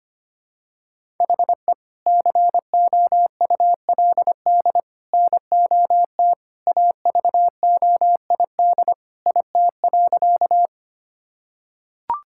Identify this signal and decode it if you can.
Morse code, heard as a keyed tone, about 25 wpm, 700 Hz. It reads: HE COULD NOT AVOID IT.
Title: Sentences from Top 1000 Words - Review Part 6 - 25wpm